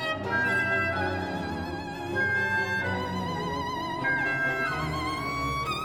Emergency Alert System (SAME) tones (header bursts), heard as harsh buzzing bursts. It is heard from 0.3 to 4.8 s.